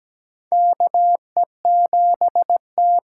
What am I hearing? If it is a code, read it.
Morse code, heard as a keyed tone, about 17 wpm, 700 Hz. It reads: KE7T
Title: USA Call Signs - 8x Character Spacing - 17wpm